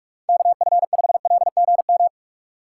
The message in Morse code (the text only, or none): KF5LGM